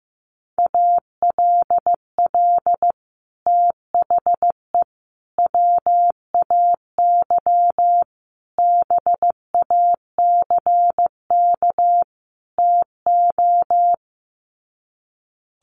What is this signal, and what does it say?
Morse code, heard as a keyed tone, about 15 wpm, 700 Hz. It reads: ALL THE WAY BACK TO
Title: Sets of 5 Words Encore 15wpm